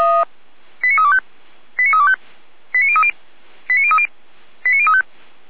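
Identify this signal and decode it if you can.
Five-tone selective call (ZVEI), heard as quick runs of five tones, tones of about 70 ms. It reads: 89325 89325 89030 89030 89035
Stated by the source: ZVEI five-tone call received by SDR.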